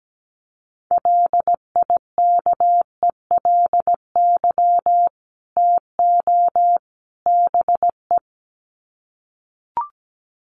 Morse code, heard as a keyed tone, about 17 wpm, 700 Hz. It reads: LIKELY TO BE
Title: Sets of 3 Words 17wpm